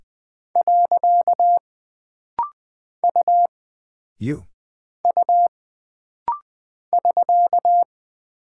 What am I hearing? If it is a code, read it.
Morse code, heard as a keyed tone, about 20 wpm, 700 Hz. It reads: . U U <SK>
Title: Single Character (Letter:Number:Punctuation:Prosign) - Rapid-Fire - 20wpm